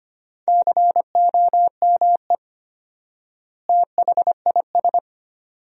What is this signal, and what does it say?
Morse code, heard as a keyed tone, about 25 wpm, 700 Hz. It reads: COME THIS
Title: 4x Word-Spacing - Sentences from Top 100 Words - 25wpm